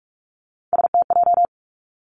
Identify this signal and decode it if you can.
Morse code, heard as a keyed tone, about 46 wpm, 700 Hz. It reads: STJ
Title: Three Challenging Letter-Number Combinations 45wpm